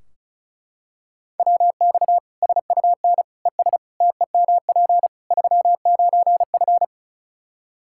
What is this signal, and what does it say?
Morse code, heard as a keyed tone, about 34 wpm, 700 Hz. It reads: WX SUN ES TEMP 39F
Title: Practice QSO - Exchange 2 - Info - 35wpm